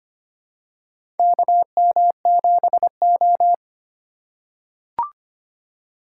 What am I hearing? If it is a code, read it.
Morse code, heard as a keyed tone, about 25 wpm, 700 Hz. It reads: KM7O
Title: USA Call Signs - 2x Character Spacing - 25wpm